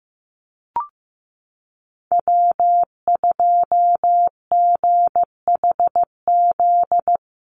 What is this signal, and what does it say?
Morse code, heard as a keyed tone, about 15 wpm, 700 Hz. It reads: W2GHZ
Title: USA Call Signs Repeated 2x 15wpm